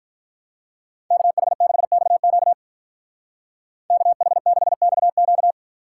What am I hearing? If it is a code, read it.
Morse code, heard as a keyed tone, about 43 wpm, 700 Hz. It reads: KH6XQ KH6XQ
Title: Speed Racing - USA Call Signs 35wpm